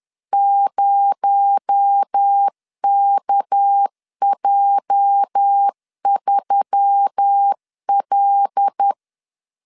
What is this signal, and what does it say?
Morse code, heard as a keyed tone, about 10 wpm, 800 Hz. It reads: 0KJ3L
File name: LEZIONI / 10 SIGNS / 10B INT 10wpm 2844